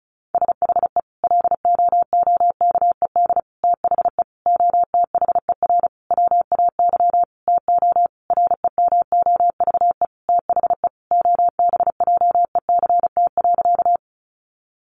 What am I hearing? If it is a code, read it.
Morse code, heard as a keyed tone, about 34 wpm, 700 Hz. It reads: SHE LOOKED THE OTHER WAY TO REMOVE THE OBJECT.
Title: Sentences from Top 800 Words 35wpm